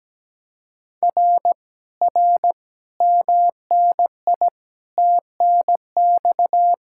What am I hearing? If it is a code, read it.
Morse code, heard as a keyed tone, about 17 wpm, 700 Hz. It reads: R R MNI TNX
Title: Speed-Racing Common QSO Phrases - 17wpm